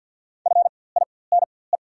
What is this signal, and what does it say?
Morse code, heard as a keyed tone, about 48 wpm, spaced as for 23 wpm, 700 Hz. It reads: FINE
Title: 4x Character Spacing - Top 500 Words - 50wpm